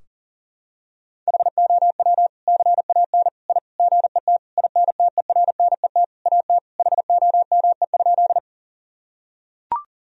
Morse code, heard as a keyed tone, about 40 wpm, 700 Hz. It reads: HOW CAN I GET INTERNET AT HOME?
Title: Sentences from Top 1000 Words - Review Part 4 - 40wpm